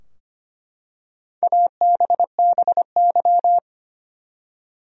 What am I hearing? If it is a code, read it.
Morse code, heard as a keyed tone, about 25 wpm, 700 Hz. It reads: ABBY
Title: USA Common Names - Max 4 Letters - 25wpm